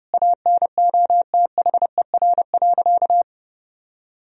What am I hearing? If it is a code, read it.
Morse code, heard as a keyed tone, about 32 wpm, 700 Hz. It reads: ANOTHER.
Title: Sentences from Top 1000 Words - Review Part 1 - 30wpm